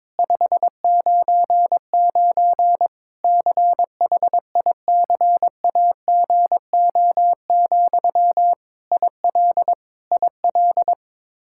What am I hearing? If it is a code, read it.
Morse code, heard as a keyed tone, about 22 wpm, 700 Hz. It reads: 599 CHICAGO, IL IL
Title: Practice QSO - Exchange 1 - Beginning - 22wpm